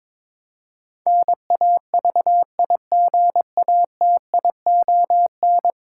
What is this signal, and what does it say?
Morse code, heard as a keyed tone, about 21 wpm, 700 Hz. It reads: NAVIGATION
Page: morse-code-ninja-practice-22wpm